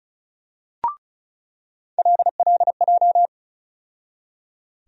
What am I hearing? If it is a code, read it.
Morse code, heard as a keyed tone, about 36 wpm, 700 Hz. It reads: LLJ
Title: USA Call Sign Postfixes - 3 Letters 35wpm